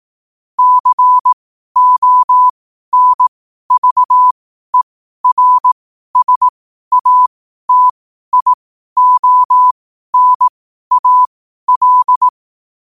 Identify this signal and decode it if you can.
Morse code, heard as a keyed tone, about 18 wpm, spaced as for 12 wpm, 1000 Hz. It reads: CONVERSATIONAL
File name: 12wpm